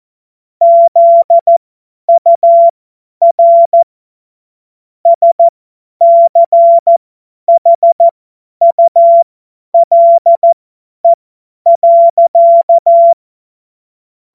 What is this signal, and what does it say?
Morse code, heard as a keyed tone, about 14 wpm, spaced as for 10 wpm, 680 Hz. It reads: ZUR SCHULE.
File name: Morsekurs / Witze / Witz 14. Februar 2023 50 F70